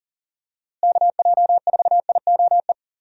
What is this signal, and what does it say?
Morse code, heard as a keyed tone, about 40 wpm, 700 Hz. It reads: KJ4IOE
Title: USA Call Signs - 6 Characters - Repeated 2x - 40wpm